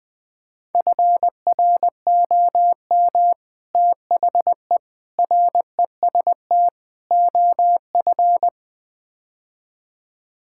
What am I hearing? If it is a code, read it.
Morse code, heard as a keyed tone, about 20 wpm, 700 Hz. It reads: FROM THE REST OF
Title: Build A Phrase - 5-Word Phrases - 20wpm